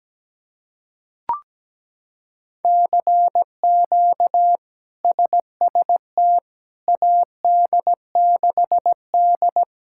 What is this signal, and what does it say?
Morse code, heard as a keyed tone, about 17 wpm, 700 Hz. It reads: CQ SST AD6D
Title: K1USN SST QSO Practice - 17wpm